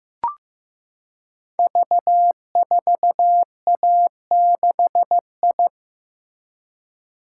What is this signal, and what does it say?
Morse code, heard as a keyed tone, about 15 wpm, 700 Hz. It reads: V4A6I
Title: Code Groups - 5 Letter:Numbers - 15wpm